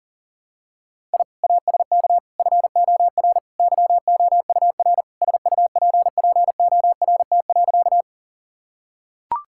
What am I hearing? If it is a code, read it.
Morse code, heard as a keyed tone, about 40 wpm, 700 Hz. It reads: I ASK FOR YOUR SUPPORT.
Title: Sentences from Top 1000 Words - Review Part 5 - 40wpm